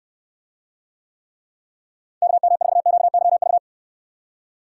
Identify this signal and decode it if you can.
Morse code, heard as a keyed tone, about 66 wpm, 700 Hz. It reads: KM4QYV